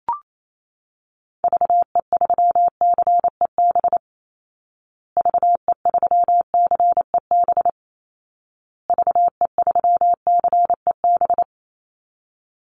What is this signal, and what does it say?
Morse code, heard as a keyed tone, about 28 wpm, 700 Hz. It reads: VE3CEB VE3CEB VE3CEB
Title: Call Signs Repeated 2x - 28wpm